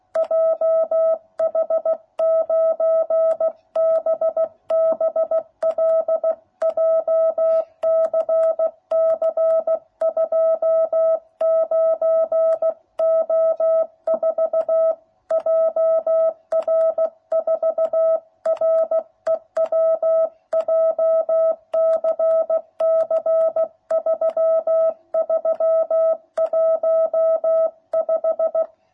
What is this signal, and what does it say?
Morse code, heard as a keyed tone, about 16 wpm, 660 Hz. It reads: JH9BBL JCC 29O4 JR4REW JCC 3315